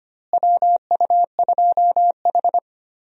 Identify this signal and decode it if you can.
Morse code, heard as a keyed tone, about 25 wpm, 700 Hz. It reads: WU2H
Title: USA Call Signs - 4 Characters - Repeated 3x - 25wpm